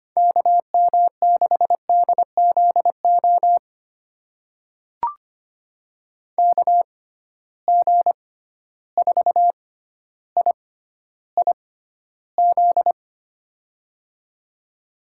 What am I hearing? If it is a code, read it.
Morse code, heard as a keyed tone, about 25 wpm, spaced as for 13 wpm, 700 Hz. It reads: KM6DZO KG4IIZ
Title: USA Call Signs - 6 Characters - 6x Character Spacing - 25wpm